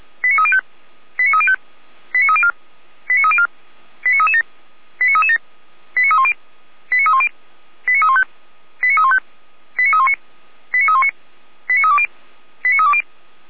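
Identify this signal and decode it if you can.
Five-tone selective call (ZVEI), heard as quick runs of five tones, tones of about 70 ms. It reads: 89305 89305 89304 89304 89337 89337 89310 89310 89315 89315 89319 89319 89320 89320
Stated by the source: ZVEI five-tone selective call received by SDR, from the Feuerwehr Orlamünde audio archive.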